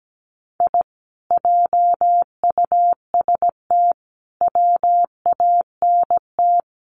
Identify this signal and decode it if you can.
Morse code, heard as a keyed tone, about 17 wpm, 700 Hz. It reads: I JUST WANT
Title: Sets of 5 Words Encore2 17wpm